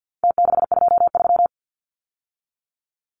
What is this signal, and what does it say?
Morse code, heard as a keyed tone, about 50 wpm, 700 Hz. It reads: T623
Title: Numbers - Contest Style - 4-Digit 50wpm